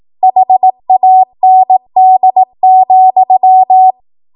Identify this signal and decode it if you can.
Morse code, heard as a keyed tone, about 18 wpm, 750 Hz. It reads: HAND,
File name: Morse / 18WPM